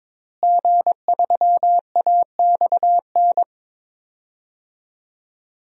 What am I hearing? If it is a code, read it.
Morse code, heard as a keyed tone, about 22 wpm, 700 Hz. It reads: G3AXN
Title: DXCC-100 Call Signs - 5 Characters - Repeated 3x - 22wpm